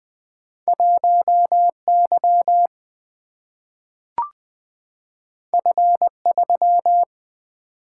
Morse code, heard as a keyed tone, about 20 wpm, 700 Hz. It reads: 1 Y F 3
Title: Single-Letter Single-Number 20wpm